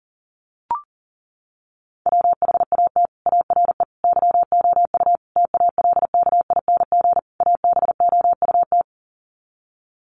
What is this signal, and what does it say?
Morse code, heard as a keyed tone, about 40 wpm, 700 Hz. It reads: WHAT ARE YOU TALKING ABOUT
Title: Sets of 5 Words 40wpm